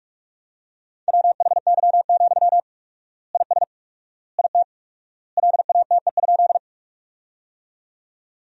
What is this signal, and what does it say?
Morse code, heard as a keyed tone, about 42 wpm, 700 Hz. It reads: WHY, IS IT LATE?
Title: 4x Word-Spacing - Sentences from Top 300 Words - 45wpm